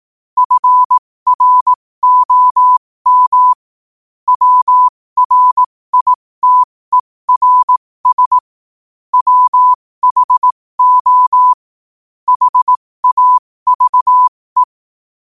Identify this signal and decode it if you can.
Morse code, heard as a keyed tone, about 18 wpm, spaced as for 15 wpm, 1000 Hz. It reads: FROM WRITERS WHO HAVE
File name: qotdfiles / 15wpm